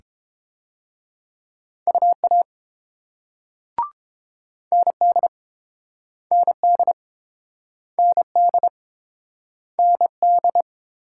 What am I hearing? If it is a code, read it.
Morse code, heard as a keyed tone, about 30 wpm, 700 Hz. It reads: UA ND ND ND ND